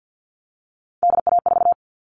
Morse code, heard as a keyed tone, about 50 wpm, 700 Hz. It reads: DA4
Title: All Cut Numbers - 3-Digit 50wpm